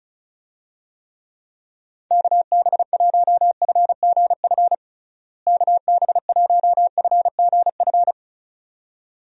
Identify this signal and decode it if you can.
Morse code, heard as a keyed tone, about 34 wpm, 700 Hz. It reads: KB1FGF KB1FGF
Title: USA Call Signs - 6 Characters - Repeated 2x - 35wpm